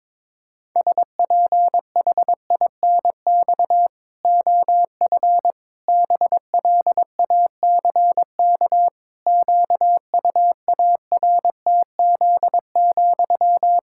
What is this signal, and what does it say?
Morse code, heard as a keyed tone, about 22 wpm, 700 Hz. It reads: SPHINX OF BLACK QUARTZ,
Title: Pangram Challenge v2 - 22wpm